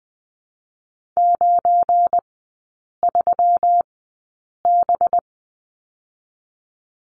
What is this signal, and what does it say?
Morse code, heard as a keyed tone, about 20 wpm, spaced as for 8 wpm, 700 Hz. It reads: 93B
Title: 205 Lesson 33e - Letter j - Three Characters 20wpm